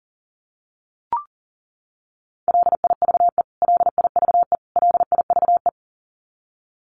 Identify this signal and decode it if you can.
Morse code, heard as a keyed tone, about 40 wpm, 700 Hz. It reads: LIVE LIVE LIVE
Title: Top 500 Words Repeated 40wpm